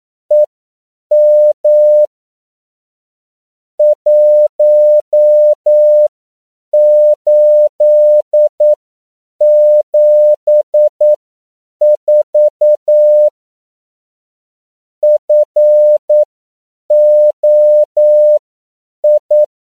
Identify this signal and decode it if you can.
Morse code, heard as a keyed tone, about 9 wpm, spaced as for 7 wpm, 600 Hz. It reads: EM 1874 FOI
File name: CW / 10PPM